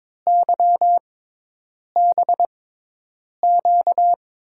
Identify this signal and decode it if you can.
Morse code, heard as a keyed tone, about 22 wpm, spaced as for 8 wpm, 700 Hz. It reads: YBQ
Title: USA Call Signs - 5 Characters - 6x Character Spacing - 22wpm